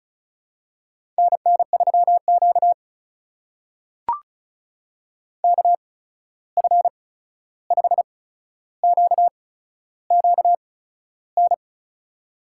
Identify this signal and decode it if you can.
Morse code, heard as a keyed tone, about 36 wpm, spaced as for 13 wpm, 700 Hz. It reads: NN3Q KF5QQN
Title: USA Call Signs - 8x Character Spacing - 35wpm